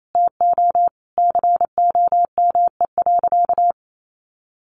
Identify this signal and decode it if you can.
Morse code, heard as a keyed tone, about 28 wpm, 700 Hz. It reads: TO COME.
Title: Sentences from Top 700 Words 28wpm